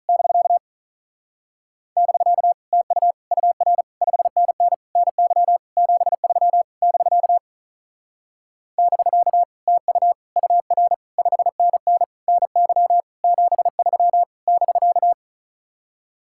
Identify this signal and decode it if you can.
Morse code, heard as a keyed tone, about 45 wpm, 700 Hz. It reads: <BK> <BK> TU UR 5NN NY 73 <BK> <BK> TU UR 5NN NY 73 <BK>